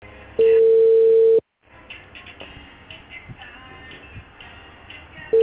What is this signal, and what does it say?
Ringback tone (China, 450 Hz).